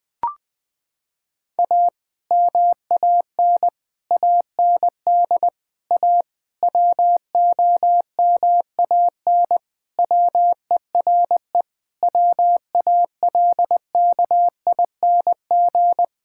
Morse code, heard as a keyed tone, about 20 wpm, 700 Hz. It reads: A MAN AND A WOMAN WERE WALKING
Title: Sentences with 5-Word Phrases - 20wpm